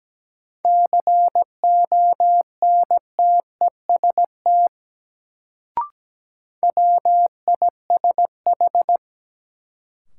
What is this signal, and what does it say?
Morse code, heard as a keyed tone, about 17 wpm, 700 Hz. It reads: CONTEST WISH
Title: Sentences from Top 1600 Words - New Word Drill - 17wpm